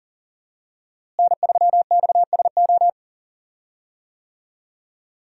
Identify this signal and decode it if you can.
Morse code, heard as a keyed tone, about 40 wpm, 700 Hz. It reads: N3XSO
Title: USA Call Signs - 5 Characters - Repeated 3x - 40wpm